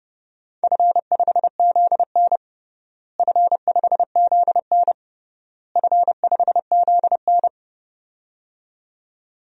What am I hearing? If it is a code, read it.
Morse code, heard as a keyed tone, about 30 wpm, 700 Hz. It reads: F5ZN F5ZN F5ZN
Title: DXCC-100 Call Signs - 4 Characters - Repeated 3x - 30wpm